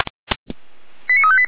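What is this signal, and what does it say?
Five-tone selective call (ZVEI), heard as quick runs of five tones, tones of about 70 ms. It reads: 89358
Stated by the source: ZVEI five-tone call received by SDR.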